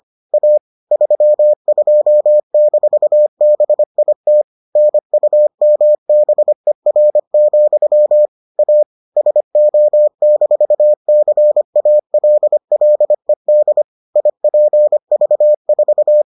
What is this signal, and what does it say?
Morse code, heard as a keyed tone, about 25 wpm, 600 Hz. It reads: A 32-BIT NUMBER, A SO-CALLED IPV4